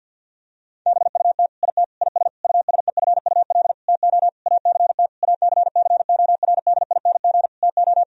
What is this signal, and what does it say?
Morse code, heard as a keyed tone, about 50 wpm, 700 Hz. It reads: BUT IT IS USEFUL TO ACT ACCORDING TO
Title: Sentences from Top 1000 Words - Review Part 4 - 50wpm